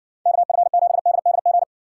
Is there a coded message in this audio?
Yes: Morse code from 0.3 to 1.6 s, heard as a keyed tone, about 60 wpm, 700 Hz. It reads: KF7DDG